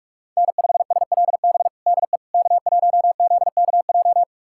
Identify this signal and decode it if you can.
Morse code, heard as a keyed tone, about 46 wpm, 700 Hz. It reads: N5SLB DE K1ZKJ
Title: USA Callsign DE Callsign 45wpm